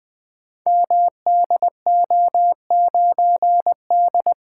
Morse code, heard as a keyed tone, about 20 wpm, 700 Hz. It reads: MDO9D
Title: Code Groups - 5 Letter:Numbers - 20wpm